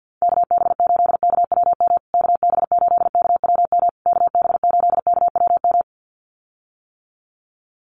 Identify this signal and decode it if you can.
Morse code, heard as a keyed tone, about 48 wpm, 700 Hz. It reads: KB8KWM KB8KWM KB8KWM
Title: USA Call Signs Repeated 3x 50wpm